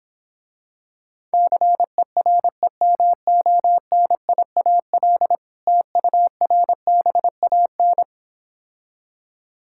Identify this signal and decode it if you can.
Morse code, heard as a keyed tone, about 26 wpm, 700 Hz. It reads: CEREMONIAL TURBAN